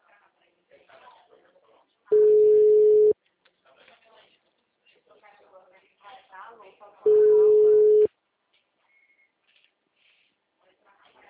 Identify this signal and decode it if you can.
Ringback tone (Europe, 425 Hz), with speech.